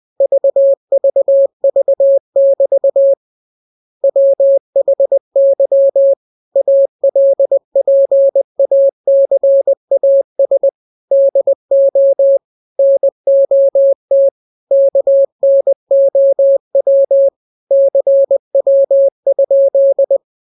Morse code, heard as a keyed tone, about 20 wpm, 550 Hz. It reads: VVV<BT> WHY ALPACAS DO NOT KNOW CW?
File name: jokes / joke 14 20 0000